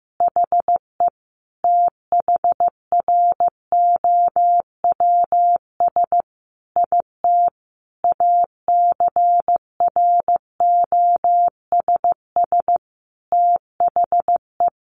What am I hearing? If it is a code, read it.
Morse code, heard as a keyed tone, about 15 wpm, 700 Hz. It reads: HE THROWS IT ACROSS THE
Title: Additional Easy Phrases 15wpm